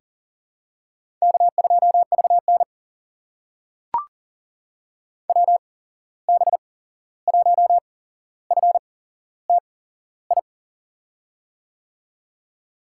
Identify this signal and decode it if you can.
Morse code, heard as a keyed tone, about 40 wpm, spaced as for 15 wpm, 700 Hz. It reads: K2VN WB1FTI
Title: USA Call Signs - 8x Character Spacing - 40wpm